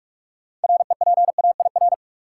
Morse code, heard as a keyed tone, about 46 wpm, 700 Hz. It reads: REPAIR
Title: Six Letter Words - 45wpm